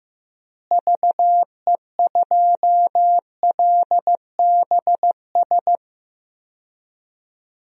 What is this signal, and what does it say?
Morse code, heard as a keyed tone, about 16 wpm, 700 Hz. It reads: VE2LBS
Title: DXCC-100 Call Signs - 6 Characters - Repeated 2x - 15wpm